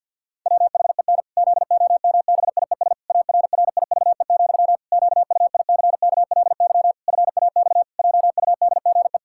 Morse code, heard as a keyed tone, about 50 wpm, 700 Hz. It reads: WHEN ZOMBIES ARRIVE, QUICKLY FAX JUDGE
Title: Pangram Challenge v2 - 50wpm